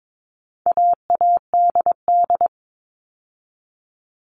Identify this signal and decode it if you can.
Morse code, heard as a keyed tone, about 22 wpm, 700 Hz. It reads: AADD
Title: All Cut Numbers - 4-Digit 22wpm